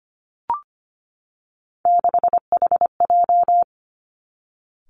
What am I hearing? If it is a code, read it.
Morse code, heard as a keyed tone, about 25 wpm, 700 Hz. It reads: 6HJ
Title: Three Challenging Letter-Number Combinations 25wpm